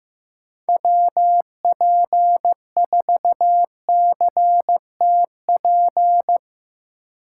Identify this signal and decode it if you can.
Morse code, heard as a keyed tone, about 15 wpm, 700 Hz. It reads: WP4CTP
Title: USA Call Signs - 6 Characters - 3x Character Spacing - 15wpm